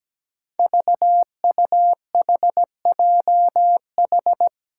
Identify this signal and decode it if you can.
Morse code, heard as a keyed tone, about 17 wpm, 700 Hz. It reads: VUHJH
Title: Five Challenging Letter-Number Combinations 17wpm